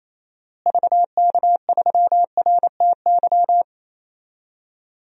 Morse code, heard as a keyed tone, about 28 wpm, 700 Hz. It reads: VK3RTY
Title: DXCC-100 Call Signs - 6 Characters - Repeated 3x - 28wpm